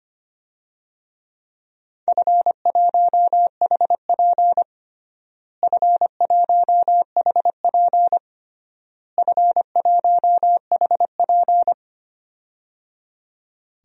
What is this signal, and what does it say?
Morse code, heard as a keyed tone, about 25 wpm, 700 Hz. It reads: F1HP F1HP F1HP
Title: DXCC-100 Call Signs - 4 Characters - Repeated 3x - 25wpm